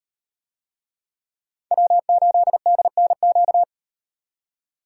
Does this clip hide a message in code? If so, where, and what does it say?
Yes: Morse code from 1.7 to 3.6 s, heard as a keyed tone, about 36 wpm, 700 Hz. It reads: W8DNQ